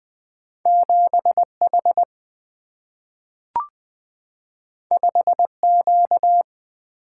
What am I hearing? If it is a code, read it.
Morse code, heard as a keyed tone, about 20 wpm, 700 Hz. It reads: 7 H 5 Q
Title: Single-Letter Single-Number 20wpm